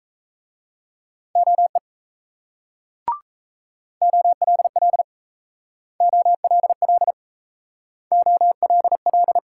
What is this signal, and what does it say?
Morse code, heard as a keyed tone, about 35 wpm, 700 Hz. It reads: OE OLL OLL OLL
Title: Speed-Racing Multi-letter Phonemes - 28wpm